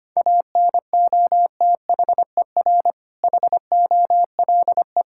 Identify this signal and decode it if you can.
Morse code, heard as a keyed tone, about 25 wpm, 700 Hz. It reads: ANOTHER HOLE
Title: Sentences from Top 1600 Words 25wpm